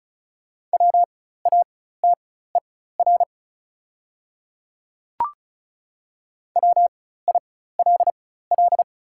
Morse code, heard as a keyed tone, about 34 wpm, spaced as for 16 wpm, 690 Hz. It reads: WATER WILL
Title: 4x Character Spacing - Top 200 Words - 35wpm